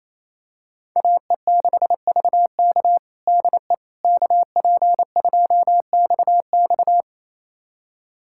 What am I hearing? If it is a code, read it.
Morse code, heard as a keyed tone, about 28 wpm, 700 Hz. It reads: AE6VK DE KP2XX
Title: USA Callsign DE Callsign - 5 Characters - 28wpm